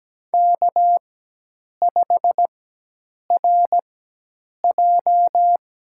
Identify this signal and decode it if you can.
Morse code, heard as a keyed tone, about 17 wpm, spaced as for 8 wpm, 700 Hz. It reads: K5RJ
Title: USA Call Signs - 4x Character Spacing - 17wpm